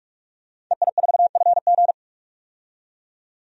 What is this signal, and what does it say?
Morse code, heard as a keyed tone, about 43 wpm, 700 Hz. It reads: EI4FG
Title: DXCC-100 Call Signs - 5 Characters - 45wpm